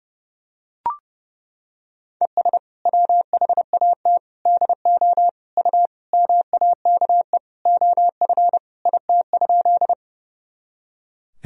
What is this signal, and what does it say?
Morse code, heard as a keyed tone, about 30 wpm, 700 Hz. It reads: ES WHAT DO U MAKE OF IT?
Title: Sentences from Top 100 Words and Abbreviations 30wpm